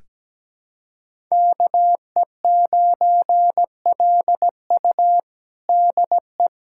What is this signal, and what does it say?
Morse code, heard as a keyed tone, about 17 wpm, 700 Hz. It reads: KE9LU DE
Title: USA Callsign DE Callsign - 5 Characters - 17wpm